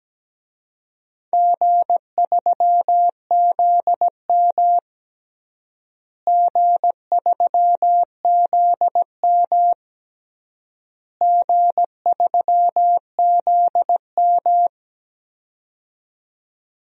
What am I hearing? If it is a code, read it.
Morse code, heard as a keyed tone, about 17 wpm, 700 Hz. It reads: G3ZM G3ZM G3ZM
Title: DXCC-100 Call Signs - 4 Characters - Repeated 3x - 17wpm